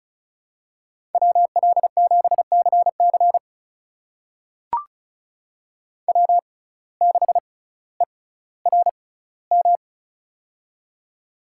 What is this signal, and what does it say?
Morse code, heard as a keyed tone, about 36 wpm, spaced as for 18 wpm, 700 Hz. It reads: WL7CC W6ERM
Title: USA Call Signs - 5 Characters - 6x Character Spacing - 35wpm